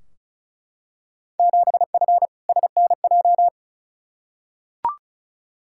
Morse code, heard as a keyed tone, about 36 wpm, 700 Hz. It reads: 7F SNJ
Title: Field Day Exchanges - 35wpm